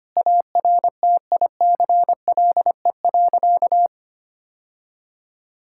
Morse code, heard as a keyed tone, about 25 wpm, 700 Hz. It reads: ARTICLE.
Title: Sentences from Top 1000 Words - Review Part 5 - 25wpm